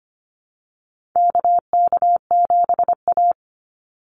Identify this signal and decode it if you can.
Morse code, heard as a keyed tone, about 25 wpm, 700 Hz. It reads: KK7A
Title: Call Signs Repeated 2x - 25wpm